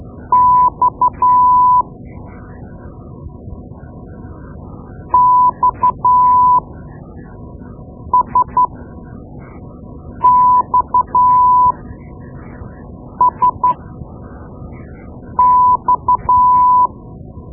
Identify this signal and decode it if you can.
Morse code, heard as a keyed tone, about 12 wpm, spaced as for 5 wpm, 1000 Hz. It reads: X XSXSX
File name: Morse Lessons / Morse Lessons / Tape-2B